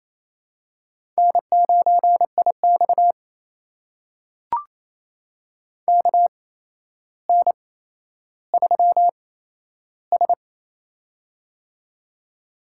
Morse code, heard as a keyed tone, about 28 wpm, spaced as for 12 wpm, 700 Hz. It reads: N9IX KN3S
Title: USA Call Signs - 4 Characters - 8x Character Spacing - 28wpm